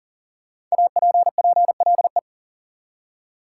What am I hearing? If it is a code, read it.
Morse code, heard as a keyed tone, about 40 wpm, 700 Hz. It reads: APPLE
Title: Five Letter Words - 40wpm